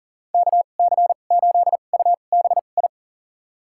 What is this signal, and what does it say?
Morse code, heard as a keyed tone, about 40 wpm, spaced as for 29 wpm, 700 Hz. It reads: KC8UBI
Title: USA Call Signs - 2x Character Spacing - 40wpm